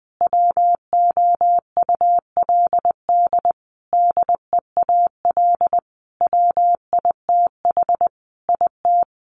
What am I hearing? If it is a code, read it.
Morse code, heard as a keyed tone, about 20 wpm, 700 Hz. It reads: WOULD DEAL WITH IT
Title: Sentences from Top 1500 Words 20wpm